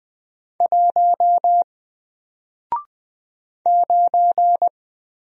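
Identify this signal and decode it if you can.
Morse code, heard as a keyed tone, about 20 wpm, 700 Hz. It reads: 1 9
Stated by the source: Single letter and number Morse drill at 20 wpm, from N4LCD.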